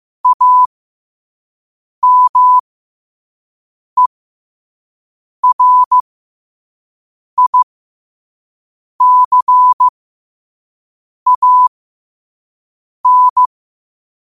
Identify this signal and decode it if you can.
Morse code, heard as a keyed tone, about 15 wpm, spaced as for 5 wpm, 1000 Hz. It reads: AMERICAN
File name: qotdfiles / 5wpm